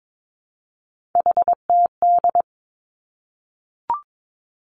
Morse code, heard as a keyed tone, about 22 wpm, 700 Hz. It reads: HTD
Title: Three Challenging Letter-Number Combinations 22wpm